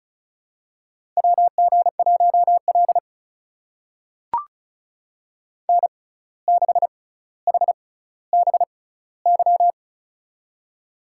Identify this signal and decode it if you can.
Morse code, heard as a keyed tone, about 34 wpm, spaced as for 17 wpm, 700 Hz. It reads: WG1L N6HBY
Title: USA Call Signs - 6x Character Spacing - 35wpm